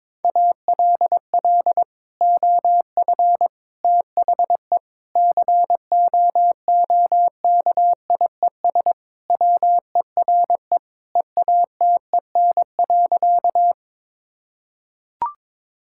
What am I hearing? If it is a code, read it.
Morse code, heard as a keyed tone, about 21 wpm, 700 Hz. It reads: ALL OF THE COOKIES WERE EATEN.
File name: practice / 22wpm / Sentences with 2-Word Phrases - 22wpm